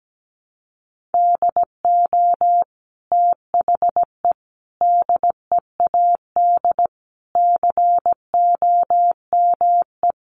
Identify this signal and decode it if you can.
Morse code, heard as a keyed tone, about 17 wpm, 700 Hz. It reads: DO THE DEAD COME
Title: Sentences from Top 1500 Words 17wpm